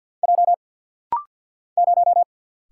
Morse code, heard as a keyed tone, about 48 wpm, 700 Hz. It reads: J 0
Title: Single Letter-Number - Rapid-Fire 50wpm